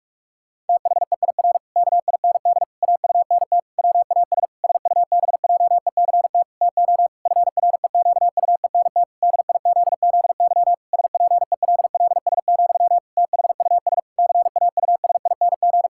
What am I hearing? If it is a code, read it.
Morse code, heard as a keyed tone, about 46 wpm, 700 Hz. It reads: THEIR KIND AUNT WAS SUBJECT TO FREQUENT DIZZY SPELLS, THUS CAUSING
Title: Pangram Challenge v2 - 45wpm